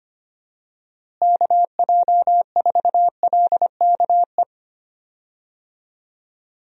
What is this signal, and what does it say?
Morse code, heard as a keyed tone, about 25 wpm, 700 Hz. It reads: KJ4LKE
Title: USA Call Signs - 6 Characters - Repeated 2x - 25wpm